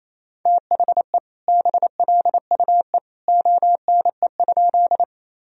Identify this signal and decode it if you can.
Morse code, heard as a keyed tone, about 27 wpm, 700 Hz. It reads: THE BLUE ONE?
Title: Sentences from Top 1000 Words - Review Part 3 - 28wpm